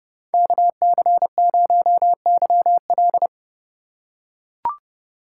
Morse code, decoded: KC0YL

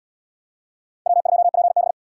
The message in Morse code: W2PB

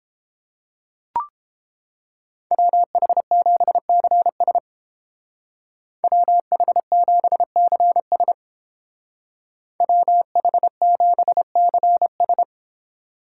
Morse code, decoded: WH7CS WH7CS WH7CS

